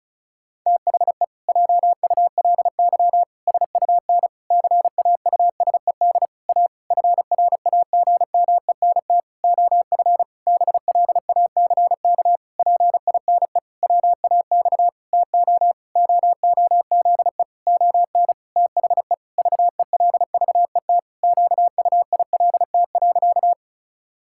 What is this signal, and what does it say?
Morse code, heard as a keyed tone, about 36 wpm, 700 Hz. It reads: THE JULY SUN CAUSED A FRAGMENT OF BLACK PINE WAX TO OOZE ON THE VELVET QUILT.
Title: Pangram Challenge v2 - 35wpm